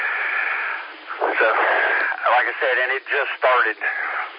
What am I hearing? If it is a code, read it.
Emergency Alert System (SAME) tones, heard as harsh buzzing bursts, with speech. It is heard from 0.6 to 3.3 s.